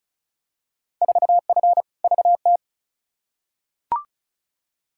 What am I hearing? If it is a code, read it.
Morse code, heard as a keyed tone, about 36 wpm, 700 Hz. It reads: 4F VT
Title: Field Day Exchanges Encore - 35wpm